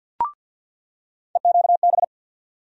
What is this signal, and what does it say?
Morse code, heard as a keyed tone, about 46 wpm, 700 Hz. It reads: EQB
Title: USA Call Sign Postfixes - 3 Letters 50wpm